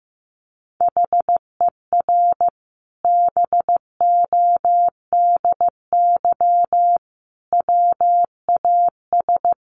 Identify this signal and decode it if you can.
Morse code, heard as a keyed tone, about 15 wpm, 700 Hz. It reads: HER BODY WAS
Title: Sentences from Top 1400 Words 15wpm